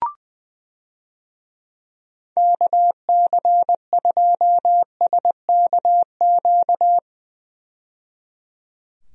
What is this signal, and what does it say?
Morse code, heard as a keyed tone, about 20 wpm, 700 Hz. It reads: KC2SKQ
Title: USA Call Signs - 6 Characters - 20wpm